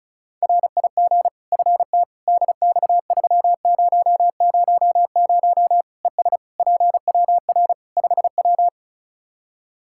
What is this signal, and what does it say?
Morse code, heard as a keyed tone, about 36 wpm, 700 Hz. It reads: RIG FT DX3000 ES PWR 5W
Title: Practice QSO - Exchange 2 - Info - 35wpm